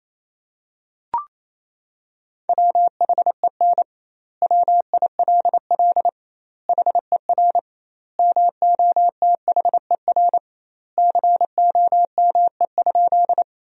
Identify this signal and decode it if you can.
Morse code, heard as a keyed tone, about 27 wpm, 700 Hz. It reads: WHEN WILL HER MOTHER COME?
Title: 2x Word-Spacing - Sentences from Top 200 Words - 28wpm